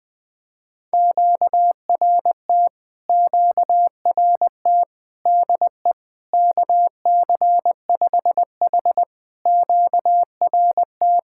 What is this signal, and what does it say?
Morse code, heard as a keyed tone, about 20 wpm, 700 Hz. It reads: QRT QRT DE KC5H QRT
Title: POTA QSO Practice - 20wpm